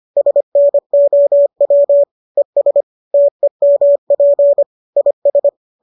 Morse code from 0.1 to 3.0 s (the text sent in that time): SNOW ES